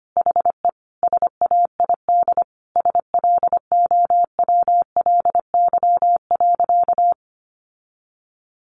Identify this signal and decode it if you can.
Morse code, heard as a keyed tone, about 25 wpm, 700 Hz. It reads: HE SAID SLOWLY.
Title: Sentences from Top 1100 Words 25wpm